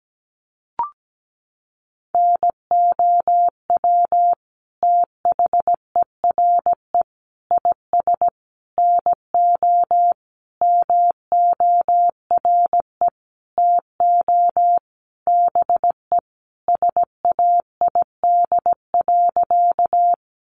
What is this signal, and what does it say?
Morse code, heard as a keyed tone, about 17 wpm, 700 Hz. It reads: NOW THERE IS NO MORE TO BE SAID.
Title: Sentences from Top 100 Words 17wpm